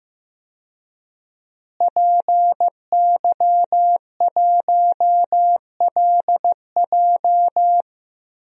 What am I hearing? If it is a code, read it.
Morse code, heard as a keyed tone, about 15 wpm, 700 Hz. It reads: PY1LJ
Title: DXCC-100 Call Signs - 5 Characters - Repeated 2x - 15wpm